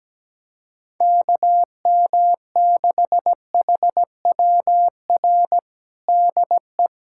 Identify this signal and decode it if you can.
Morse code, heard as a keyed tone, about 17 wpm, 700 Hz. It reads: KM6HWR DE
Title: USA Callsign DE Callsign - 6 Characters - 17wpm